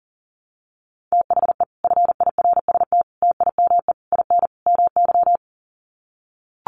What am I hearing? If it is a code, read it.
Morse code, heard as a keyed tone, about 40 wpm, 700 Hz. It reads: THE FIRST TIME IN MY
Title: Sets of 5 Words Encore 40wpm